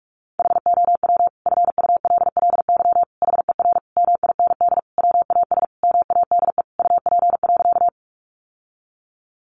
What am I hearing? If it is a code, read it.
Morse code, heard as a keyed tone, about 44 wpm, 700 Hz. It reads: HOW FULLY HER MIND WAS MADE UP.
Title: Sentences from Top 1300 Words 45wpm